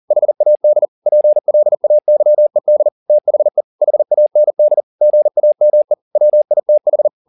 Morse code, heard as a keyed tone, about 40 wpm, 600 Hz. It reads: HAD PLAYED THE HAND GAME WITH